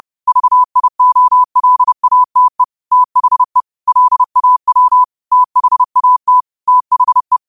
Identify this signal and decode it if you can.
Morse code, heard as a keyed tone, about 30 wpm, 1000 Hz. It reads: VIOLATE THE LAW THAT THE